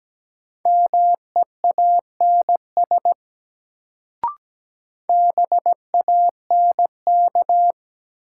Morse code, heard as a keyed tone, about 17 wpm, 700 Hz. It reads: MEANS BANK
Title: Sets of 5 Words - New Word Drill - 17wpm